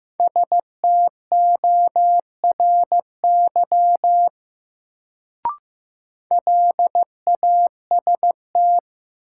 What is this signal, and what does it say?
Morse code, heard as a keyed tone, about 15 wpm, 700 Hz. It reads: STORY LAST
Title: Sentences from Top 300 Words - New Word Drill - 15wpm